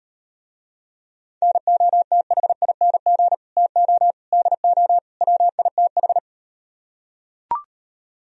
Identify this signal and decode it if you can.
Morse code, heard as a keyed tone, about 38 wpm, 700 Hz. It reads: NOTHING TO DO WITH